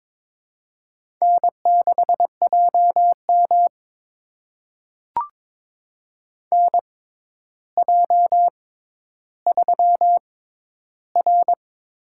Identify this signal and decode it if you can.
Morse code, heard as a keyed tone, about 21 wpm, spaced as for 11 wpm, 700 Hz. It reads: N6JM NJ3R
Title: USA Call Signs - 4 Characters - 6x Character Spacing - 22wpm